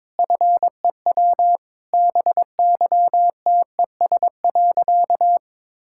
Morse code, heard as a keyed tone, about 22 wpm, 690 Hz. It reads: FEW BYTES.
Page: morse-code-ninja-practice-22wpm